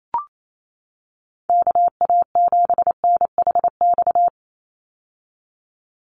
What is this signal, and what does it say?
Morse code, heard as a keyed tone, about 28 wpm, 700 Hz. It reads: KA7NHX
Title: USA Call Signs 28wpm